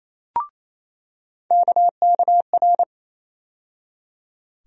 Morse code, heard as a keyed tone, about 28 wpm, 700 Hz. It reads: KKR
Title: USA Call Sign Postfixes - 3 Letters 28wpm